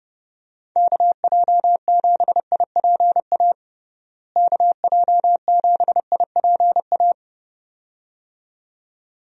Morse code, heard as a keyed tone, about 30 wpm, 700 Hz. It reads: KJ7IPA KJ7IPA
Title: USA Call Signs - 6 Characters - Repeated 2x - 30wpm